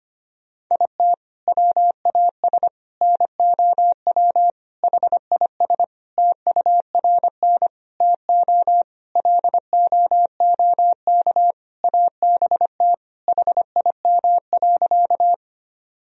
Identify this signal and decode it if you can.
Morse code, heard as a keyed tone, about 25 wpm, 700 Hz. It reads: IT WAS NOW HIS TURN TO LOOK ABT HIM.
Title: Sentences from Top 200 Words and Abbreviations 25wpm